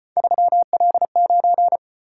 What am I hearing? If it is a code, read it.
Morse code, heard as a keyed tone, about 32 wpm, 700 Hz. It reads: 3L9